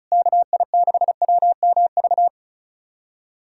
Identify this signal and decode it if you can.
Morse code, heard as a keyed tone, about 34 wpm, 700 Hz. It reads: KI6WMV
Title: USA Callsign DE Callsign 35wpm